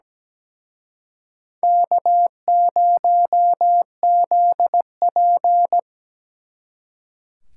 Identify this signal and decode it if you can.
Morse code, heard as a keyed tone, about 20 wpm, 700 Hz. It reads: K0ZP
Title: Speed Racing - USA 1x2 Call Signs 17wpm